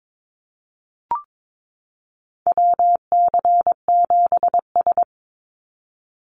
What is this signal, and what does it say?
Morse code, heard as a keyed tone, about 22 wpm, 700 Hz. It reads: WC7S
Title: Call Signs - 22wpm